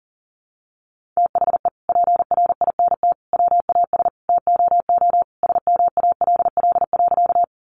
Morse code, heard as a keyed tone, about 40 wpm, 700 Hz. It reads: THE PRINT WAS TOO SMALL.
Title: Sentences from Top 1400 Words 40wpm